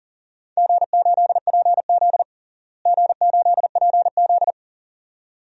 Morse code, heard as a keyed tone, about 40 wpm, 700 Hz. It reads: G8PZ G8PZ
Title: DXCC-100 Call Signs - 4 Characters - Repeated 2x - 40wpm